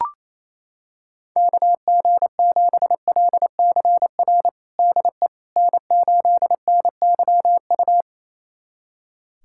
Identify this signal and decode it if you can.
Morse code, heard as a keyed tone, about 28 wpm, 700 Hz. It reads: KG7LCR DE N8NYU
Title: USA Callsign DE Callsign 28wpm